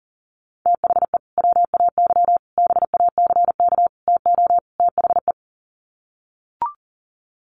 Morse code, heard as a keyed tone, about 40 wpm, 700 Hz. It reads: THE WAY BACK TO THE
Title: Sets of 5 Words Encore2 40wpm